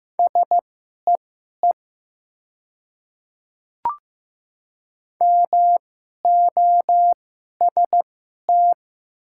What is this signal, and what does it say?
Morse code, heard as a keyed tone, about 15 wpm, spaced as for 11 wpm, 700 Hz. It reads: SEE MOST
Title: 2x Character Spacing - Top 200 Words - 15wpm